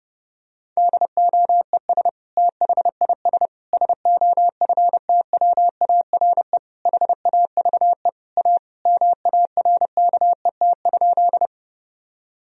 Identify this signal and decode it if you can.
Morse code, heard as a keyed tone, about 30 wpm, 700 Hz. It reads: DOES THIS SOFTWARE HAVE A MARKET?
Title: Sentences from Top 1000 Words - Review Part 3 - 30wpm